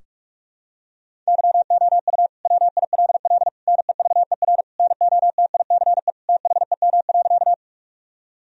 Morse code, heard as a keyed tone, about 41 wpm, 700 Hz. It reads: YOU WILL NEVER NOTICE THEM.
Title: Sentences from Top 1000 Words - Review Part 2 - 45wpm